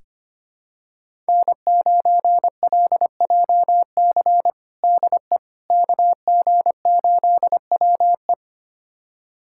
Morse code, heard as a keyed tone, about 25 wpm, 700 Hz. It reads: N9LJC DE KG8WE
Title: USA Callsign DE Callsign - 5 Characters - 25wpm